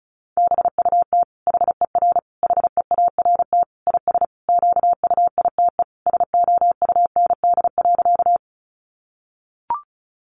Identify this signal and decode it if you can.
Morse code, heard as a keyed tone, about 34 wpm, 700 Hz. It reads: BUT HER HEART IS QUITE SOUND.
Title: Sentences from Top 1200 Words 35wpm